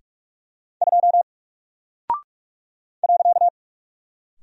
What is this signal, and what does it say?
Morse code, heard as a keyed tone, about 44 wpm, 700 Hz. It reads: 2 .
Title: Single Character (Letter:Number:Punctuation:Prosign) Alt 45wpm